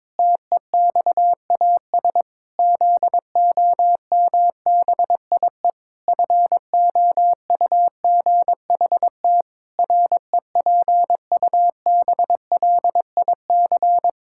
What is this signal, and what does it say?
Morse code, heard as a keyed tone, about 22 wpm, 700 Hz. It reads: TEXAS ZOMBIE FOUGHT REPUBLIC
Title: Pangram Challenge v2 - 22wpm